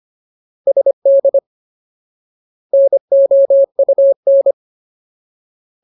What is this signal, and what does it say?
Morse code, heard as a keyed tone, about 25 wpm, 550 Hz. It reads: SD NOUN
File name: Ultimate Word List / 25WPM Ultimate